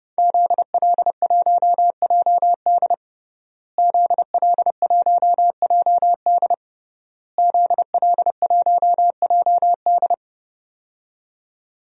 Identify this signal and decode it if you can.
Morse code, heard as a keyed tone, about 28 wpm, 700 Hz. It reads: ZL1JD ZL1JD ZL1JD
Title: DXCC-100 Call Signs - 5 Characters - Repeated 3x - 30wpm